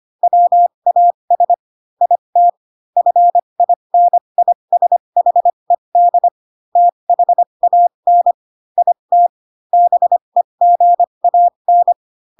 Morse code, heard as a keyed tone, about 25 wpm, spaced as for 22 wpm, 700 Hz. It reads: WAS IT FINISHED THAN IT BEGAN
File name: Pinocchio / Pinocchio ch3 18 25 ik1hge